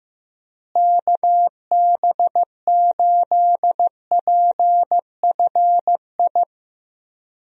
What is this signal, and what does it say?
Morse code, heard as a keyed tone, about 15 wpm, 700 Hz. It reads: KB8PFI
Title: USA Call Signs - 6 Characters - Repeated 2x - 15wpm